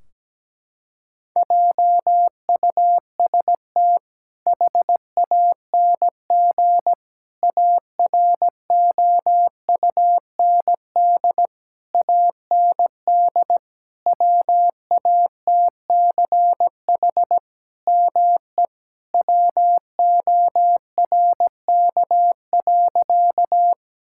Morse code, heard as a keyed tone, about 17 wpm, 700 Hz. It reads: JUST HANG AROUND AND WATCH ME WORK.
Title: Sentences from Top 1600 Words 17wpm